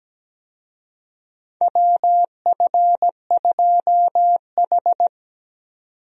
Morse code, heard as a keyed tone, about 17 wpm, 700 Hz. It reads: WF2H
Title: USA Call Signs - 4 Characters - 17wpm